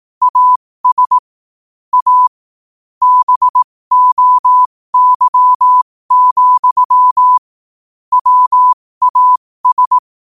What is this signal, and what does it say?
Morse code, heard as a keyed tone, about 18 wpm, spaced as for 15 wpm, 1000 Hz. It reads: AS A BOY, WAS
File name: qotdfiles / 15wpm